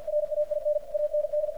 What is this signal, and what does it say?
Morse code, heard as a keyed tone, about 52 wpm, 600 Hz, with radio noise. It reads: RAIN AND
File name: archived / CRSnet